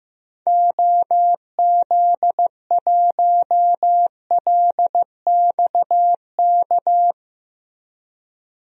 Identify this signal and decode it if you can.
Morse code, heard as a keyed tone, about 15 wpm, 700 Hz. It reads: OZ1LXK
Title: DXCC-100 Call Signs - 6 Characters - Repeated 2x - 15wpm